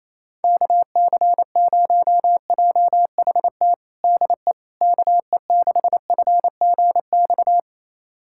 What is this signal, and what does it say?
Morse code, heard as a keyed tone, about 28 wpm, 700 Hz. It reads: KC0JHT DE KE6FGX
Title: USA Callsign DE Callsign - 6 Characters - 28wpm